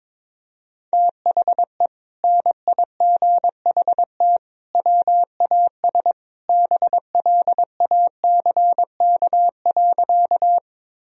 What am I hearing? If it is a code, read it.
Morse code, heard as a keyed tone, about 22 wpm, 700 Hz. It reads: THE NIGHT WAS BLACK.
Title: Sentences from Top 1000 Words - Review Part 2 - 22wpm